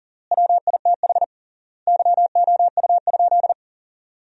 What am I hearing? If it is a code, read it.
Morse code, heard as a keyed tone, about 40 wpm, 700 Hz. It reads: WITH YOU?
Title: 3x Word-Spacing - Sentences from Top 500 Words - 40wpm